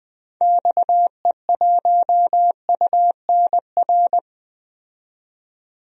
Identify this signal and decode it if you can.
Morse code, heard as a keyed tone, about 20 wpm, 700 Hz. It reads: XE1UNR
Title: DXCC-100 Call Signs - 6 Characters - 20wpm